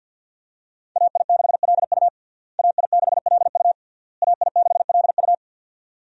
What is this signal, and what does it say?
Morse code, heard as a keyed tone, about 52 wpm, 690 Hz. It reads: AI6LU AI6LU AI6LU
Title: USA Call Signs - 5 Characters - Repeated 3x - 50wpm